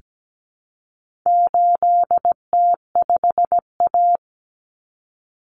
Morse code, heard as a keyed tone, about 17 wpm, 700 Hz. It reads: 8T5A
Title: Common Cut Numbers - 4-Digit 17wpm